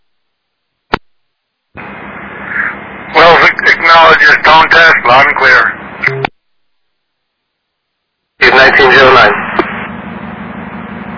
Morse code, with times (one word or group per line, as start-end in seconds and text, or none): none